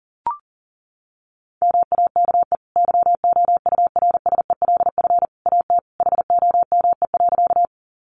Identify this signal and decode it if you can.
Morse code, heard as a keyed tone, about 40 wpm, 700 Hz. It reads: MAKE YOURSELF AT HOME.
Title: Sentences from Top 600 Words 40wpm